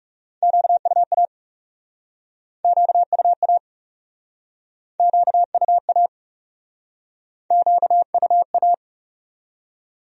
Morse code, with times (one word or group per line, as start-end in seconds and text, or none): 0.4-1.2 QUA
2.6-3.6 QUA
5.0-6.1 QUA
7.5-8.7 QUA